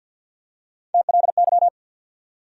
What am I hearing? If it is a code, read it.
Morse code, heard as a keyed tone, about 52 wpm, 700 Hz. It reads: TLY
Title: USA Call Sign Postfixes - 3 Letters 50wpm